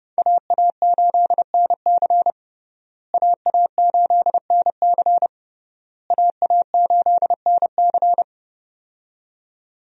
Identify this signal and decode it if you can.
Morse code, heard as a keyed tone, about 32 wpm, 700 Hz. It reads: AA8NC AA8NC AA8NC
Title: USA Call Signs - 5 Characters - Repeated 3x - 30wpm